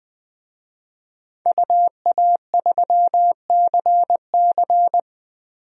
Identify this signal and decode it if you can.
Morse code, heard as a keyed tone, about 20 wpm, 700 Hz. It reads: UA3CC
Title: DXCC-100 Call Signs - 5 Characters - Repeated 2x - 20wpm